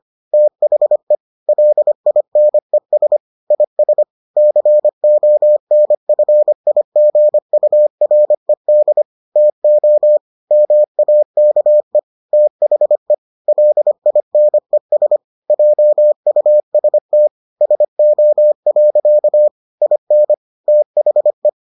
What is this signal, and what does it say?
Morse code, heard as a keyed tone, about 25 wpm, 600 Hz. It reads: THE LINES IS CONFIGURED TO MAKE THE LINES JUST SO. IN THE